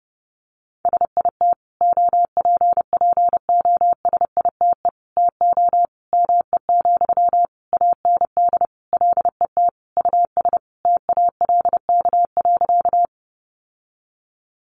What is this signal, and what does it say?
Morse code, heard as a keyed tone, about 30 wpm, 700 Hz. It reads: SIT OPPOSITE TO ME, AND LET US TALK.
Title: Sentences from Top 1300 Words 30wpm